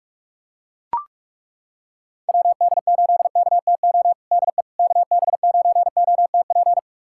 Morse code, heard as a keyed tone, about 42 wpm, 700 Hz. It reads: WD8KTO DE KB9OTP
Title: USA Callsign DE Callsign - 6 Characters - 45wpm